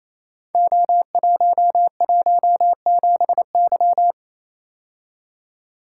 Morse code, heard as a keyed tone, about 29 wpm, 700 Hz. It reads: O117Y
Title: Code Groups - 5 Letter:Numbers - 28wpm